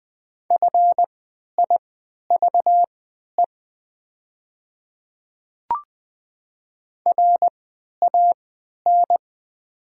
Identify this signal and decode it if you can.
Morse code, heard as a keyed tone, about 20 wpm, spaced as for 11 wpm, 700 Hz. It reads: FIVE RAN
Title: 3x Character Spacing - Top 500 Words - 20wpm